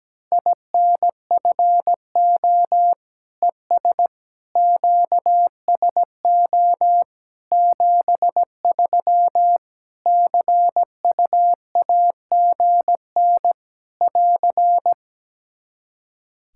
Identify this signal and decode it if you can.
Morse code, heard as a keyed tone, about 17 wpm, 700 Hz. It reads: INFO ES QSO 73 CUAGN <AR>
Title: Practice QSO - Exchange 4 - Ending - 17wpm